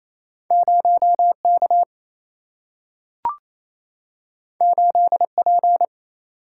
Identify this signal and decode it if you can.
Morse code, heard as a keyed tone, about 28 wpm, 700 Hz. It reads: 0 K 8 P
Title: Single-Letter Single-Number 28wpm